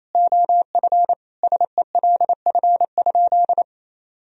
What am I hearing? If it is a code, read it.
Morse code, heard as a keyed tone, about 27 wpm, 700 Hz. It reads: OF SELF?
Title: Sentences from Top 1000 Words - Review Part 1 - 28wpm